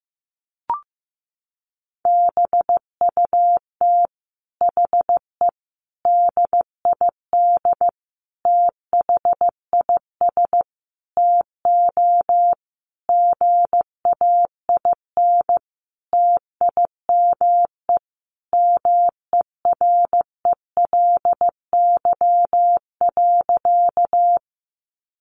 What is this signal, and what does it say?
Morse code, heard as a keyed tone, about 15 wpm, 700 Hz. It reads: BUT HE DID THIS TO GAIN TIME MERELY.
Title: Sentences from Top 1400 Words 15wpm